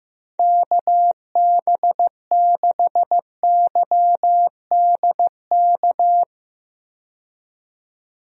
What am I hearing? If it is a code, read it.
Morse code, heard as a keyed tone, about 15 wpm, 700 Hz. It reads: KB6YDK
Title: USA Call Signs - 6 Characters - 15wpm